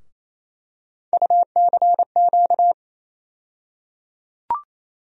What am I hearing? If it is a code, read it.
Morse code, heard as a keyed tone, about 28 wpm, 700 Hz. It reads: UCQ
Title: USA Call Sign Postfixes - 3 Letters 28wpm